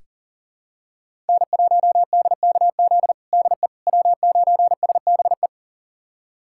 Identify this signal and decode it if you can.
Morse code, heard as a keyed tone, about 37 wpm, 700 Hz. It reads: N1DKZ DE W9SBE
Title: USA Callsign DE Callsign - 5 Characters - 40wpm